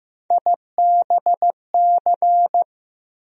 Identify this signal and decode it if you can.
Morse code, heard as a keyed tone, about 15 wpm, 700 Hz. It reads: IBC
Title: USA Call Sign Postfixes - 3 Letters 15wpm